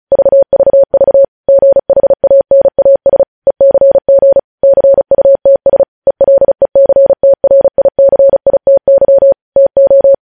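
Morse code, heard as a keyed tone, about 36 wpm, 550 Hz. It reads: VVV GHANAS ECG CUTS ELECTRICITY TO
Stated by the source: Morse code practice sending news headlines at 35 wpm.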